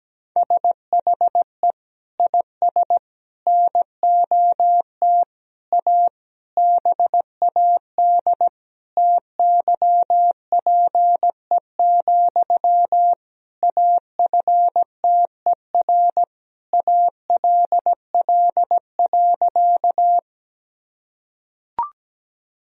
Morse code, heard as a keyed tone, about 17 wpm, 700 Hz. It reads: SHE IS NOT A BAD TYPE, AFTER ALL.
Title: Sentences from Top 1000 Words - Review Part 3 - 17wpm